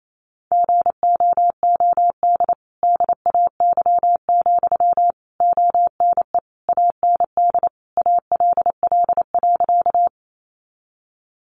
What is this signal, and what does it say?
Morse code, heard as a keyed tone, about 28 wpm, 700 Hz. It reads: GOOD DAY, ONE AND ALL.
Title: Sentences from Top 200 Words 28wpm